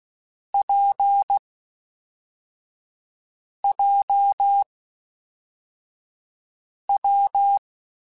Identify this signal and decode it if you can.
Morse code, heard as a keyed tone, about 16 wpm, 800 Hz. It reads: P J W
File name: Difficult Combos / Difficult Combos-mp3s / WPJ WPJ1 @10